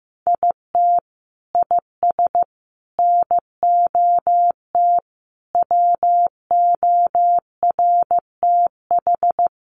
Morse code, heard as a keyed tone, about 15 wpm, 700 Hz. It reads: IT IS NOT WORTH
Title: Sentences from Top 800 Words 15wpm